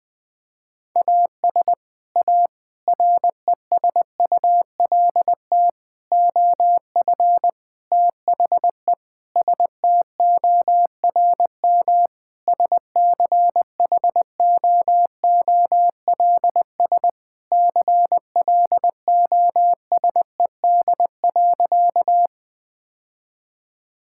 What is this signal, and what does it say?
Morse code, heard as a keyed tone, about 20 wpm, 700 Hz. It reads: AS A RESULT OF THE STORM SCHOOLS CLOSED.
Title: Sentences with 5-Word Phrases - Spoken Beginning - 20wpm